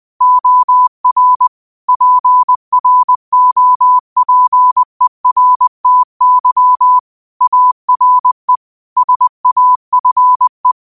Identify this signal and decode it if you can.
Morse code, heard as a keyed tone, about 20 wpm, 1000 Hz. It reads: OR PROPERTY ARE SAFE
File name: qotdfiles / 20wpm